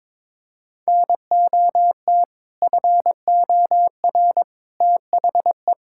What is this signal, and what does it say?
Morse code, heard as a keyed tone, about 22 wpm, 700 Hz. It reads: NOT FOR THE
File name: practice / 22wpm / Sets of 6 Words - 22wpm